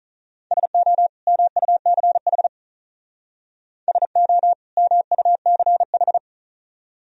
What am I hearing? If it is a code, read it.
Morse code, heard as a keyed tone, about 45 wpm, 700 Hz. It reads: SO MUCH SO MUCH